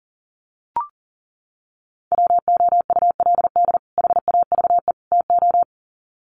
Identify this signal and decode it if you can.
Morse code, heard as a keyed tone, about 40 wpm, 700 Hz. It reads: WOULD HAVE TO
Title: Sets of 3 Words 40wpm